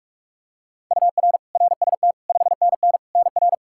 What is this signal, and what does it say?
Morse code, heard as a keyed tone, about 42 wpm, 700 Hz. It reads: UR RST 5NN NR
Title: Practice QSO - Exchange 1 - Beginning - 45wpm